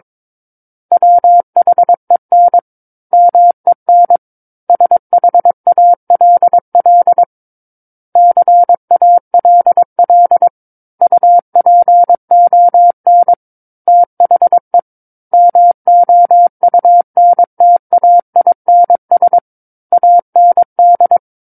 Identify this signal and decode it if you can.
Morse code, heard as a keyed tone, about 22 wpm, 700 Hz. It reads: WHEN MEN SHALL CALL UPON THE MOUNTAINS AND
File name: Book One / 13-Chapter 13 - 22 WPM